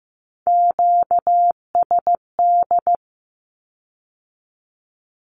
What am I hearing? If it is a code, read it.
Morse code, heard as a keyed tone, about 15 wpm, 700 Hz. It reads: QSD
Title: Q-Codes with Spoken Meaning - 15wpm